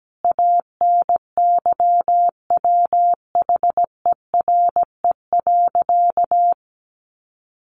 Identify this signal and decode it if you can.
Morse code, heard as a keyed tone, about 17 wpm, 700 Hz. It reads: ANYWHERE.
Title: Sentences from Top 1400 Words 17wpm